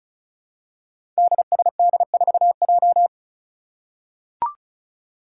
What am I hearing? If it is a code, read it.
Morse code, heard as a keyed tone, about 36 wpm, 700 Hz. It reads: DSD4J
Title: Five Challenging Letter-Number Combinations 35wpm